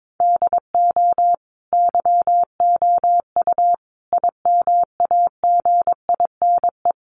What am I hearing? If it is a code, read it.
Morse code, heard as a keyed tone, about 22 wpm, 700 Hz. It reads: DO YOU IMAGINE
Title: Sentences from Top 1400 Words 22wpm